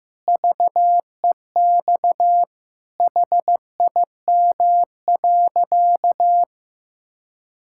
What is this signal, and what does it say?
Morse code, heard as a keyed tone, about 15 wpm, 700 Hz. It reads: VEX HIM.
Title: Pangram Challenge v2 - 15wpm